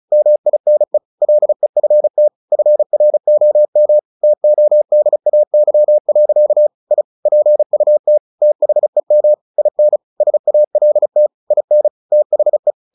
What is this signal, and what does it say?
Morse code, heard as a keyed tone, about 34 wpm, 600 Hz. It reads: MINE LEFT FROM TODAY. I PUT THEM IN SALT IN THE
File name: the old man and the sea / the old man and the sea 1 35wpm